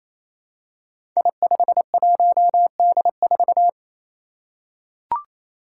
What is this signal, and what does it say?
Morse code, heard as a keyed tone, about 27 wpm, 700 Hz. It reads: I51D4
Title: Code Groups - 5 Letter:Numbers - 28wpm